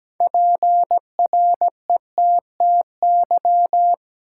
Morse code, heard as a keyed tone, about 17 wpm, 700 Hz. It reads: PRETTY
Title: Sentences from Top 1000 Words - New Word Drill - 17wpm